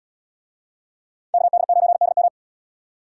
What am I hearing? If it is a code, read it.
Morse code, heard as a keyed tone, about 77 wpm, 700 Hz. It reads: KD9DM